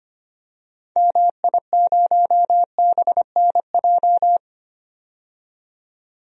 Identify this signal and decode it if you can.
Morse code, heard as a keyed tone, about 25 wpm, 690 Hz. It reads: MI0BNJ
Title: DXCC-100 Call Signs - 6 Characters - 25wpm